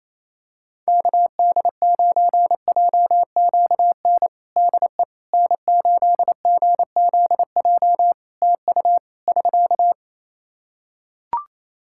Morse code, heard as a keyed tone, about 27 wpm, 700 Hz. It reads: KD9JQN DE N8GZJ TU <SK>
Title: K1USN SST QSO Practice 28wpm